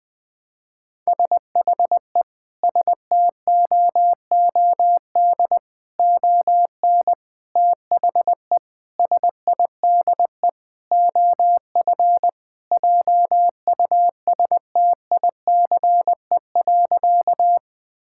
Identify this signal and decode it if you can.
Morse code, heard as a keyed tone, about 19 wpm, 700 Hz. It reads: SHE STOOD ON THE SIDE OF JUSTICE.
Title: Sentences with 4-Word Phrases - Spoken Beginning - 20wpm